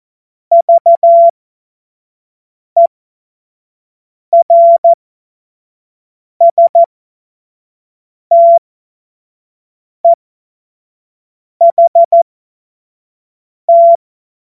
Morse code, heard as a keyed tone, about 14 wpm, spaced as for 5 wpm, 680 Hz. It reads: VERSTEHT
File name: Morsekurs / Witze / Witz 09. Februar 2021, 25 F70